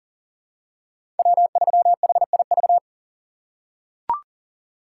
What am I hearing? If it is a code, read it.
Morse code, heard as a keyed tone, about 40 wpm, 700 Hz. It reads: W3HIV
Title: USA Call Signs - 5 Characters - 2x Character Spacing - 40wpm